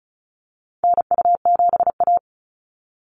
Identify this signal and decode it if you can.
Morse code, heard as a keyed tone, about 34 wpm, 700 Hz. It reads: NU7A
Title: NA SOTA Activator Callsigns 35wpm